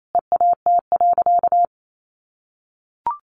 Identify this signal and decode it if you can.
Morse code, heard as a keyed tone, about 28 wpm, 700 Hz. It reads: EAT.
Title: Sentences from Top 1300 Words 28wpm